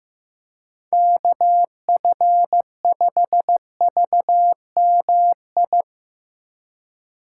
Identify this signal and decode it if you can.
Morse code, heard as a keyed tone, about 15 wpm, 700 Hz. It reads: KF5VMI
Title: USA Call Signs - 6 Characters - 2x Character Spacing - 15wpm